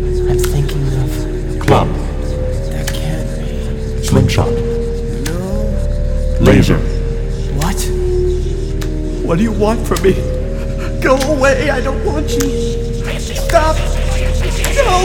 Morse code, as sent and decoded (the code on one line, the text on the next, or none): none